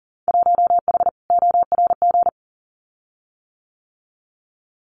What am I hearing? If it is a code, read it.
Morse code, heard as a keyed tone, about 38 wpm, 700 Hz. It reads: 1H ORG
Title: Winter Field Day Exchanges 40wpm